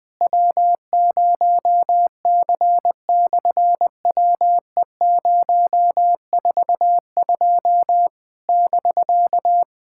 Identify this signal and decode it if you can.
Morse code, heard as a keyed tone, about 20 wpm, 700 Hz. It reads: W0C/WE042 <BK>
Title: SOTA QSO Practice 20wpm